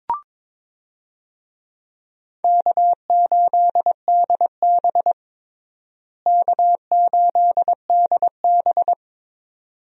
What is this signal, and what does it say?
Morse code, heard as a keyed tone, about 22 wpm, 700 Hz. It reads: K8DB K8DB
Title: USA Call Signs - 4 Characters - Repeated 2x - 22wpm